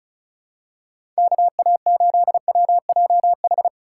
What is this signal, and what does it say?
Morse code, heard as a keyed tone, about 36 wpm, 700 Hz. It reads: KA8WJH
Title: USA Call Signs - 6 Characters - 2x Character Spacing - 35wpm